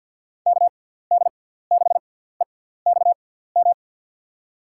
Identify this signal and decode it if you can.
Morse code, heard as a keyed tone, about 53 wpm, 700 Hz. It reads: K D 6 E X M
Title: USA Call Signs - 6 Characters - 6x Character Spacing - 50wpm